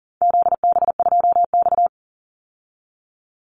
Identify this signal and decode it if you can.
Morse code, heard as a keyed tone, about 40 wpm, 700 Hz. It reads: ZB2X
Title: Call Signs Repeated 2x - 40wpm